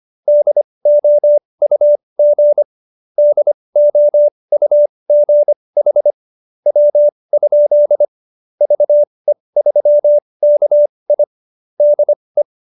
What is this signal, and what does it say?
Morse code, heard as a keyed tone, about 25 wpm, spaced as for 19 wpm, 590 Hz. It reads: DOUG DOUG HW? VE3KI DE